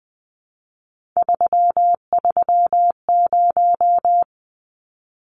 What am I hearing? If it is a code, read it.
Morse code, heard as a keyed tone, about 20 wpm, 700 Hz. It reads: 330
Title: Numbers - 3-Digit 20wpm